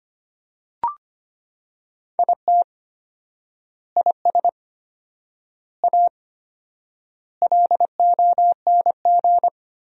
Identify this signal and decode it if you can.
Morse code, heard as a keyed tone, about 25 wpm, 700 Hz. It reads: IT IS A LONG
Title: 4x Word-Spacing - Sentences from Top 300 Words - 25wpm